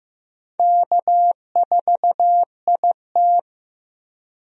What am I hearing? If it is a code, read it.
Morse code, heard as a keyed tone, about 16 wpm, 700 Hz. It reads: K4IT
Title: USA Call Signs - 4 Characters - Repeated 3x - 15wpm